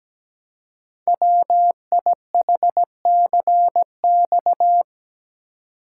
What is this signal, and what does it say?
Morse code, heard as a keyed tone, about 17 wpm, 700 Hz. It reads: WIHCX
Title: Code Groups - 5 Letters - 17wpm